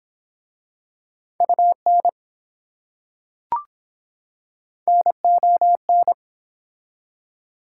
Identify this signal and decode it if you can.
Morse code, heard as a keyed tone, about 23 wpm, 700 Hz. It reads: UN NON